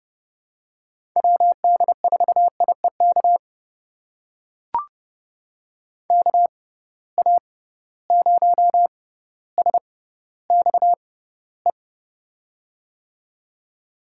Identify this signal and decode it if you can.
Morse code, heard as a keyed tone, about 31 wpm, spaced as for 15 wpm, 700 Hz. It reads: WD4IEK KA0SXE
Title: USA Call Signs - 6x Character Spacing - 30wpm